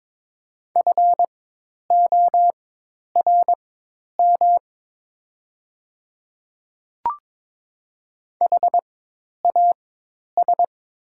Morse code, heard as a keyed tone, about 22 wpm, spaced as for 10 wpm, 700 Hz. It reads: FORM HAS
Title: 4x Character Spacing - Top 200 Words - 22wpm